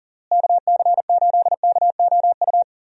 Morse code, heard as a keyed tone, about 40 wpm, 700 Hz. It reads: KC8KOU
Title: USA Call Signs - 6 Characters - Repeated 3x - 40wpm